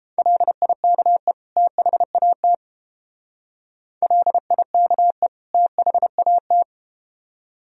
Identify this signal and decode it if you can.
Morse code, heard as a keyed tone, about 31 wpm, 700 Hz. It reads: LIKE THAT LIKE THAT